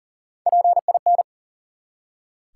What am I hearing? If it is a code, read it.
Morse code, heard as a keyed tone, about 38 wpm, 700 Hz. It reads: PIN
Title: Three Letter Words - 40wpm